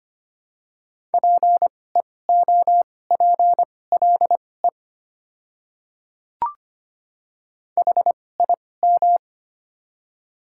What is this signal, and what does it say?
Morse code, heard as a keyed tone, about 25 wpm, spaced as for 18 wpm, 700 Hz. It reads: PEOPLE HIM
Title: 2x Character Spacing - Top 100 Words - 25wpm